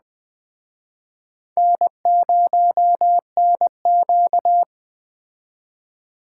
Morse code, heard as a keyed tone, about 20 wpm, 700 Hz. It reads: N0NQ
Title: Speed Racing - USA 1x2 Call Signs 17wpm